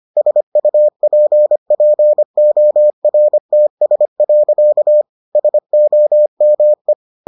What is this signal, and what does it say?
Morse code, heard as a keyed tone, about 25 wpm, 600 Hz. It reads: SUPPORTS. SOME